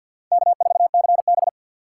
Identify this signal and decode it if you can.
Morse code, heard as a keyed tone, about 52 wpm, 700 Hz. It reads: K4CB
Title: USA Call Signs - 3x Character Spacing - 50wpm